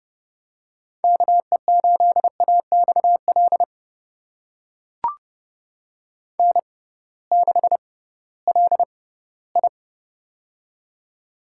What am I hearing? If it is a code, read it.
Morse code, heard as a keyed tone, about 30 wpm, spaced as for 18 wpm, 700 Hz. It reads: KE8AXL N6LI
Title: USA Call Signs - 6x Character Spacing - 30wpm